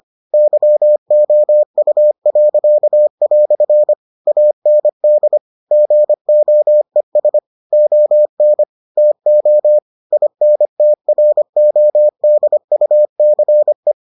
Morse code, heard as a keyed tone, about 25 wpm, 600 Hz. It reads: YOU." AND GOES ON TO INTRODUCE